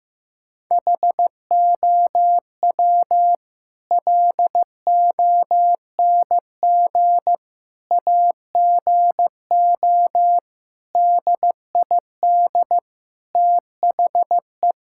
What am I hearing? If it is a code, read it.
Morse code, heard as a keyed tone, about 15 wpm, 700 Hz. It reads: HOW LONG AGO DID THE
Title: Sentences from Top 1000 Words - Review Part 6 - 15wpm